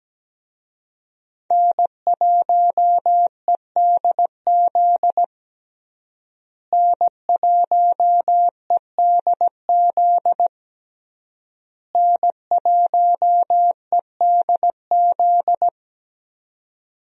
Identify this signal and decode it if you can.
Morse code, heard as a keyed tone, about 17 wpm, 700 Hz. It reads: N1EDZ N1EDZ N1EDZ
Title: USA Call Signs - 5 Characters - Repeated 3x - 17wpm